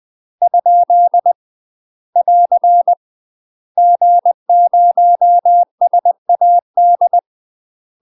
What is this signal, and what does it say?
Morse code, heard as a keyed tone, about 20 wpm, 700 Hz. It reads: ? <AR> G0SAD
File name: UK MASTER QSOs TRAINING / 20WPM / UK MASTER QSOs 0013